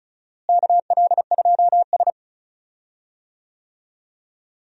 Morse code, heard as a keyed tone, about 36 wpm, 700 Hz. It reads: KL2S
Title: USA Call Signs - 4 Characters - Repeated 2x - 35wpm